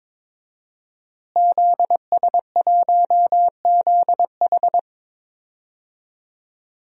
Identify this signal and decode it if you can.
Morse code, heard as a keyed tone, about 21 wpm, 700 Hz. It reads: ZS1ZH
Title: DXCC-100 Call Signs - 5 Characters - Repeated 2x - 22wpm